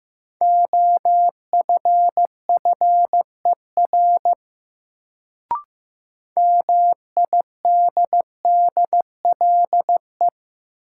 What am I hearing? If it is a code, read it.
Morse code, heard as a keyed tone, about 15 wpm, 700 Hz. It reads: OFFER MIDDLE
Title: Sentences from Top 900 Words - New Word Drill - 15wpm